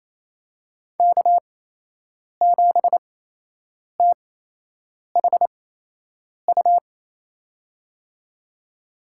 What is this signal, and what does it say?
Morse code, heard as a keyed tone, about 28 wpm, spaced as for 8 wpm, 700 Hz. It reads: K7THU
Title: USA Call Signs - 5 Characters - 8x Character Spacing - 28wpm